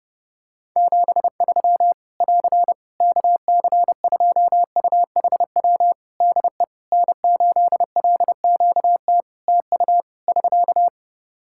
Morse code, heard as a keyed tone, about 30 wpm, 700 Hz. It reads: 73 <AR> KC2UHW DE N8LQT TU <SK>
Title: Practice QSO - Exchange 4 - Ending - 30wpm